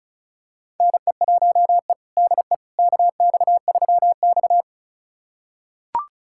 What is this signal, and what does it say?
Morse code, heard as a keyed tone, about 36 wpm, 700 Hz. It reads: NE1E DE KX3X
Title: USA Callsign DE Callsign - 4 Characters - 35wpm